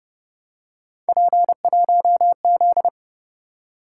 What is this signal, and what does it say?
Morse code, heard as a keyed tone, about 30 wpm, 700 Hz. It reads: P1Z
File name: Speed-Racing - Three Letter-Number Combinations - 25wpm